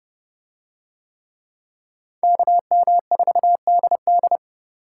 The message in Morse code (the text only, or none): KM4DD